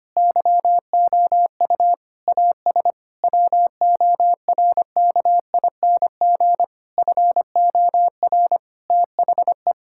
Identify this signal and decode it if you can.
Morse code, heard as a keyed tone, about 25 wpm, 700 Hz. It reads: YOU AS WORKING FOR THE
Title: Sentences from Top 1000 Words - Review Part 6 - 25wpm